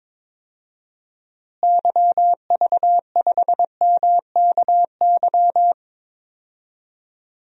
Morse code, heard as a keyed tone, about 21 wpm, 700 Hz. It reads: YV5MKY
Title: DXCC-100 Call Signs - 6 Characters - Repeated 3x - 22wpm